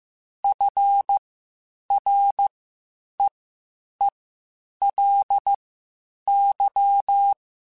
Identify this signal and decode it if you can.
Morse code, heard as a keyed tone, about 15 wpm, spaced as for 8 wpm, 800 Hz. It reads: FREELY
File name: Quotes Spaced / Quotes Spaced-mp3s / Advice sp @13